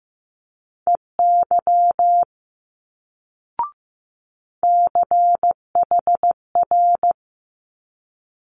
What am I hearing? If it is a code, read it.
Morse code, heard as a keyed tone, about 15 wpm, 700 Hz. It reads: EY CHR
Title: Multi-letter Phonemes Quick Fire 15wpm